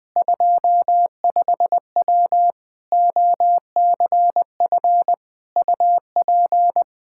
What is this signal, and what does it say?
Morse code, heard as a keyed tone, about 20 wpm, 700 Hz. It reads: 25W OCF UP
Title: K1USN SST QSO Practice 20wpm